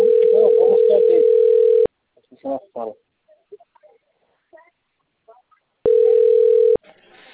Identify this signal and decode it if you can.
Ringback tone (North America), 440 + 480 Hz, with speech.
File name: out- - - - - .30260